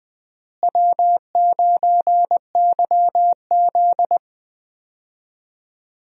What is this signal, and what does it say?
Morse code, heard as a keyed tone, about 21 wpm, 700 Hz. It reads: W9YZ
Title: USA Call Signs - 4 Characters - Repeated 3x - 20wpm